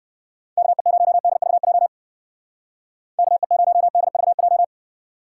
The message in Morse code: KE0DVJ KE0DVJ

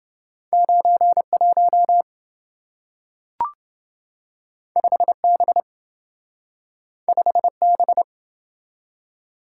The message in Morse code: 91 5B 5B

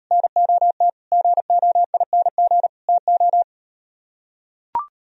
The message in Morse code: NOT GOING TO

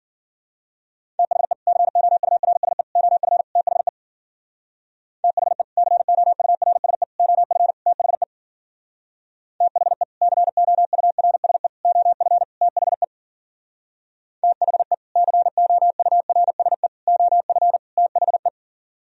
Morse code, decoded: THE COURSE OF THE THE COURSE OF THE THE COURSE OF THE THE COURSE OF THE

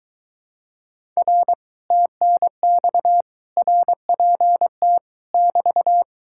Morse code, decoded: R TNX RPT <BT>

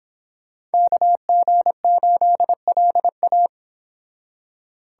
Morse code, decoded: KG8LA

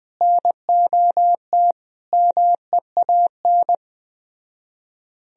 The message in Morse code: NOT MEAN

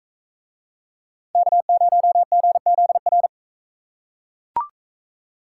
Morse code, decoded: K0GZR

